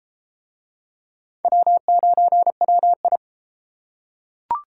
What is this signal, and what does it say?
Morse code, heard as a keyed tone, about 31 wpm, 700 Hz. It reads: W9WI